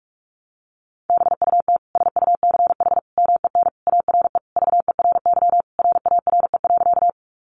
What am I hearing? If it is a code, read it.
Morse code, heard as a keyed tone, about 44 wpm, 700 Hz. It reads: BUT SUCH MEN ARE VERY RARE.
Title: Sentences from Top 1100 Words 45wpm